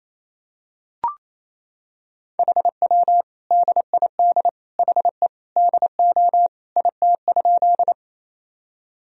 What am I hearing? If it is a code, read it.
Morse code, heard as a keyed tone, about 28 wpm, 700 Hz. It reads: HW DID HE DO IT?
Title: Sentences from Top 100 Words and Abbreviations 28wpm